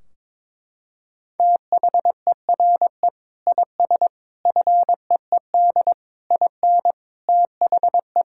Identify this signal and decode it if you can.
Morse code, heard as a keyed tone, about 22 wpm, 700 Hz. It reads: THERE IS FEED IN THE
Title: Sentences from Top 1600 Words 22wpm